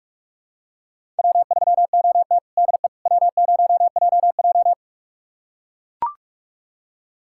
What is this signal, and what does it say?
Morse code, heard as a keyed tone, about 44 wpm, 700 Hz. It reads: W3OT DE W0JJ
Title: USA Callsign DE Callsign - 4 Characters - 45wpm